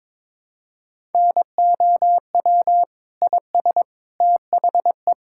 Morse code, heard as a keyed tone, about 22 wpm, 700 Hz. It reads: NOW IS THE
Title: Pangram Challenge v2 - 22wpm